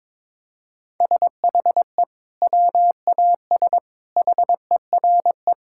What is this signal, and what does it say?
Morse code, heard as a keyed tone, about 22 wpm, 700 Hz. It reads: SHE WAS HERE
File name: practice / 22wpm / Sentences with 2-Word Phrases - 22wpm